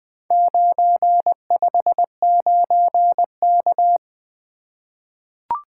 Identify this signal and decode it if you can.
Morse code, heard as a keyed tone, about 20 wpm, 700 Hz. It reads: 959K
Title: Four Letter-Number Combinations - 20wpm